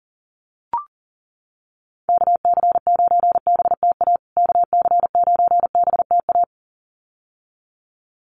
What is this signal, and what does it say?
Morse code, heard as a keyed tone, about 40 wpm, 700 Hz. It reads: KC9BTA KC9BTA
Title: USA Call Signs Repeated 2x 40wpm